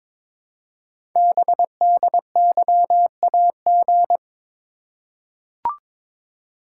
Morse code, heard as a keyed tone, about 23 wpm, 700 Hz. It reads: BDYAG
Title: Code Groups - 5 Letters - 22wpm